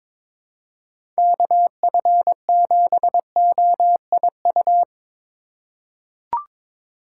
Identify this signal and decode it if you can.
Morse code, heard as a keyed tone, about 22 wpm, 700 Hz. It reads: KF7OIU
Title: USA Call Signs - 6 Characters - 4x Character Spacing - 22wpm